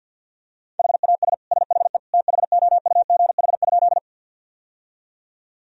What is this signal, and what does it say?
Morse code, heard as a keyed tone, about 52 wpm, 700 Hz. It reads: HAS SHE THOUGH?
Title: Sentences from Top 1000 Words - Review Part 2 - 50wpm